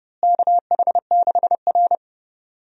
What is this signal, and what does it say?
Morse code, heard as a keyed tone, about 30 wpm, 700 Hz. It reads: KH6R
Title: USA Call Signs - 4 Characters - Repeated 3x - 30wpm